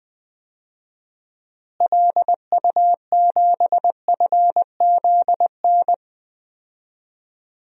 Morse code, heard as a keyed tone, about 20 wpm, 700 Hz. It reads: LU7FZN
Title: DXCC-100 Call Signs - 6 Characters - Repeated 2x - 20wpm